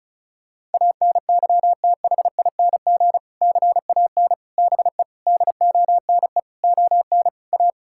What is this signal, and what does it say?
Morse code, heard as a keyed tone, about 36 wpm, 700 Hz. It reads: ANYTHING CAN BE DONE ON A
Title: Sentences from Top 1000 Words - Review Part 5 - 35wpm